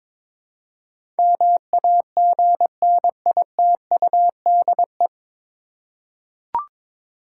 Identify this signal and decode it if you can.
Morse code, heard as a keyed tone, about 22 wpm, 700 Hz. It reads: MAGNITUDE
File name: 9 Letter Words - First 3 Letters Given - 22wpm